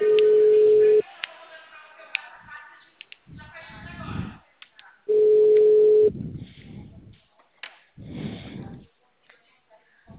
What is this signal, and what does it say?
Ringback tone (Europe, 425 Hz).
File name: out- -2002- - - .42183